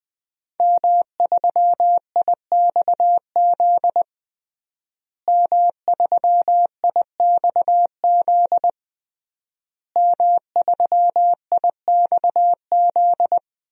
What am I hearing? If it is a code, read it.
Morse code, heard as a keyed tone, about 20 wpm, 700 Hz. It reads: M3IXZ M3IXZ M3IXZ
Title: DXCC-100 Call Signs - 5 Characters - Repeated 3x - 20wpm